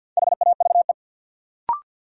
Morse code, heard as a keyed tone, about 48 wpm, 700 Hz. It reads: HAVE
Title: Sentences from Top 100 Words - New Word Drill - 50wpm